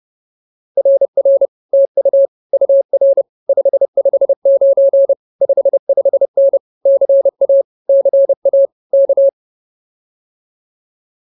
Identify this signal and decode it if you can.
Morse code, heard as a keyed tone, about 28 wpm, 560 Hz. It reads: RR TU UR 559 55N CA CA K